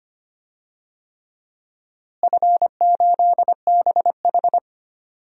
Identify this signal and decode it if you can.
Morse code, heard as a keyed tone, about 25 wpm, 700 Hz. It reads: F8BH
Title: DXCC-100 Call Signs - 4 Characters - Repeated 3x - 25wpm